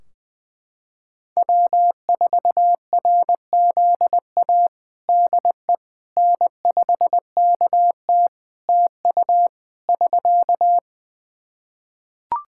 Morse code, heard as a keyed tone, about 20 wpm, 700 Hz. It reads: W4RZA DE N5KT TU <SK>
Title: Practice QSO - Exchange 4 - Ending - 20wpm